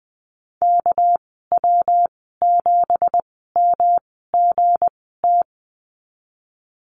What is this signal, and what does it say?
Morse code, heard as a keyed tone, about 20 wpm, spaced as for 15 wpm, 700 Hz. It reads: KW7MGT
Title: 231 Lesson 37g - Letter x - Call Signs 20wpm